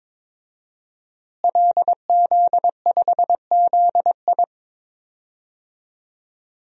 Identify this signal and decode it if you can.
Morse code, heard as a keyed tone, about 22 wpm, 700 Hz. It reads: LZ5ZI
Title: DXCC-100 Call Signs - 5 Characters - Repeated 2x - 22wpm